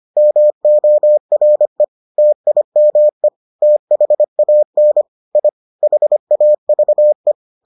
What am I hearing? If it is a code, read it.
Morse code, heard as a keyed tone, about 25 wpm, 600 Hz. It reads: MORE TIME THAN I HAVE